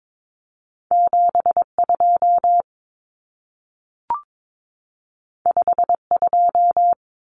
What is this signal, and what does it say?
Morse code, heard as a keyed tone, about 22 wpm, 700 Hz. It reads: 72 52
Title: Common Cut Numbers - 2-Digit 22wpm